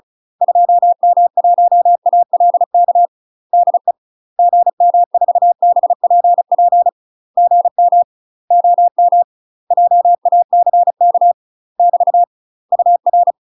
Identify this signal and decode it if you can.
Morse code, heard as a keyed tone, about 36 wpm, 700 Hz. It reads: 2M1ALK DE GM4BPP GM OM JACK <BT> UR
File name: UK MASTER QSOs TRAINING / 35WPM / UK MASTER QSOs 0009